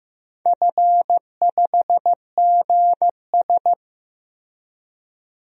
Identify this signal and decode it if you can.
Morse code, heard as a keyed tone, about 15 wpm, 700 Hz. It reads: F5GS
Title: DXCC-100 Call Signs - 4 Characters - Repeated 2x - 15wpm